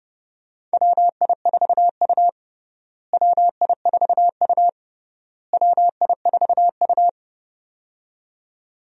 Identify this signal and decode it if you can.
Morse code, heard as a keyed tone, about 30 wpm, 700 Hz. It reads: WI4U WI4U WI4U
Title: USA Call Signs - 4 Characters - Repeated 3x - 30wpm